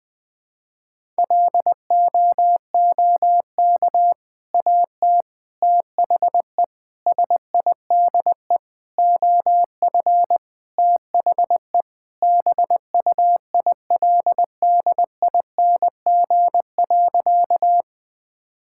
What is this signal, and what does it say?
Morse code, heard as a keyed tone, about 20 wpm, 700 Hz. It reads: LOOK AT THE SIDE OF THE BUILDING.
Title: Sentences with 3-Word Phrases - 20wpm